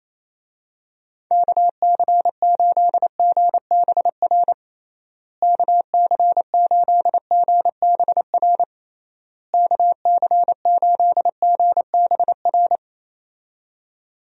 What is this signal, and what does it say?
Morse code, heard as a keyed tone, about 28 wpm, 700 Hz. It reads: KC8GBR KC8GBR KC8GBR
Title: USA Call Signs - 6 Characters - Repeated 3x - 28wpm